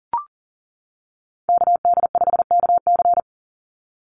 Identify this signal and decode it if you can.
Morse code, heard as a keyed tone, about 40 wpm, 700 Hz. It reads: KD5KC
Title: NA SOTA Activator Callsigns 40wpm